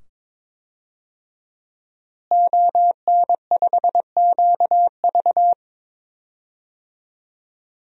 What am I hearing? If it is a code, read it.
Morse code, heard as a keyed tone, about 22 wpm, 700 Hz. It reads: ON5QV
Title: DXCC-100 Call Signs - 5 Characters - Repeated 2x - 22wpm